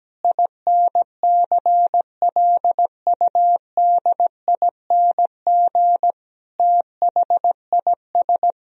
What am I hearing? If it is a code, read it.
Morse code, heard as a keyed tone, about 17 wpm, 700 Hz. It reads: INCLUDING THIS
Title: Sentences from Top 1000 Words - Review Part 4 - 17wpm